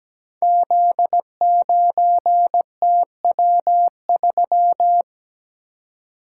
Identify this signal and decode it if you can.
Morse code, heard as a keyed tone, about 17 wpm, 700 Hz. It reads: Z9TW3
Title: Code Groups - 5 Letter:Numbers - 17wpm